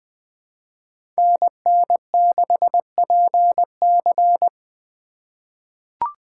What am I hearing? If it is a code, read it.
Morse code, heard as a keyed tone, about 20 wpm, 700 Hz. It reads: NN6PC
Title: USA Call Signs - 5 Characters - 3x Character Spacing - 20wpm